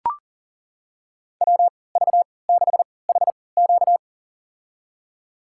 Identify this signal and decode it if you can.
Morse code, heard as a keyed tone, about 40 wpm, spaced as for 23 wpm, 700 Hz. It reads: WV6HQ
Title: USA Call Signs - 5 Characters - 3x Character Spacing - 40wpm